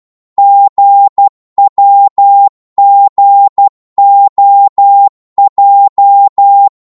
Morse code, heard as a keyed tone, about 12 wpm, 800 Hz. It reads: GWGOJ
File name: Morse / ltr-12wpm